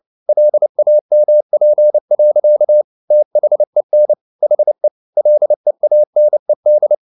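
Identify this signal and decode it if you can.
Morse code, heard as a keyed tone, about 29 wpm, 600 Hz. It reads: LAMP. THEN HE LEANED
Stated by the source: Morse code at 29 wpm, sending Ken Follett's novel Never in English, from CW med Gnister.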